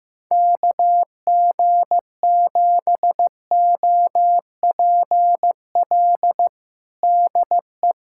Morse code, heard as a keyed tone, about 15 wpm, 700 Hz. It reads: KG7OPL DE
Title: USA Callsign DE Callsign 15wpm